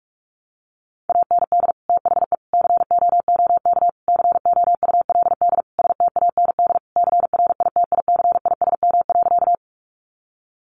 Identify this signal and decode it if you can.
Morse code, heard as a keyed tone, about 44 wpm, 700 Hz. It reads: AND THE COOK COULD STAND CRITICISM.
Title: Sentences from Top 1300 Words 45wpm